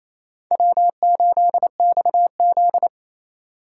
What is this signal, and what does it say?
Morse code, heard as a keyed tone, about 36 wpm, 700 Hz. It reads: W8XZ